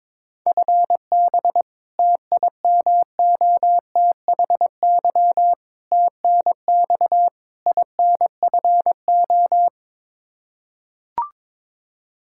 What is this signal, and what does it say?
Morse code, heard as a keyed tone, about 22 wpm, 700 Hz. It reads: FB TIMOTHY TNX INFO
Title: K1USN SST QSO Practice 22wpm